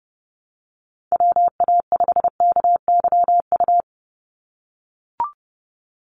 Morse code, heard as a keyed tone, about 30 wpm, 700 Hz. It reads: WA5KYU
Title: USA Call Signs 30wpm